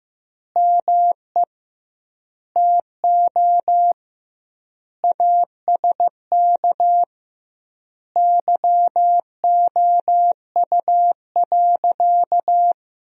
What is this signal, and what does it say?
Morse code, heard as a keyed tone, about 15 wpm, 700 Hz. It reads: ME TO ASK YOU.
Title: 2x Word-Spacing - Sentences from Top 400 Words - 15wpm